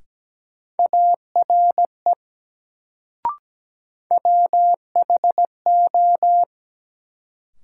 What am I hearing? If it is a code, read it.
Morse code, heard as a keyed tone, about 17 wpm, 700 Hz. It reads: ARE WHO
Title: Sentences from Top 100 Words - New Word Drill - 17wpm